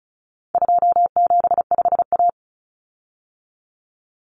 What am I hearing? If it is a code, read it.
Morse code, heard as a keyed tone, about 34 wpm, 700 Hz. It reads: 275A
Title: Common Cut Numbers - 4-Digit 35wpm